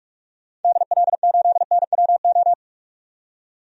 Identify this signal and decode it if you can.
Morse code, heard as a keyed tone, about 45 wpm, 700 Hz. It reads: DL8NWO
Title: DXCC-100 Call Signs - 6 Characters - 45wpm